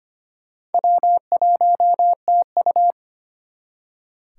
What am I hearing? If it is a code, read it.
Morse code, heard as a keyed tone, about 32 wpm, 700 Hz. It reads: W1TU